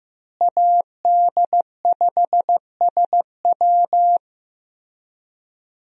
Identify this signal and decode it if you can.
Morse code, heard as a keyed tone, about 15 wpm, 700 Hz. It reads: AD5SW
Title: USA Call Signs - 5 Characters - Repeated 3x - 15wpm